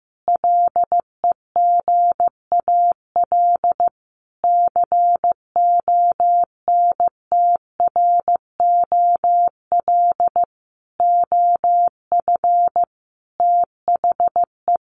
Morse code, heard as a keyed tone, about 15 wpm, 700 Hz. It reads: LEGAL CONTROL OF THE
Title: Sentences from Top 1200 Words 15wpm